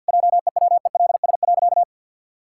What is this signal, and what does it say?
Morse code, heard as a keyed tone, about 50 wpm, 700 Hz. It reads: JEWELS.
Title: Pangram Challenge v2 - 50wpm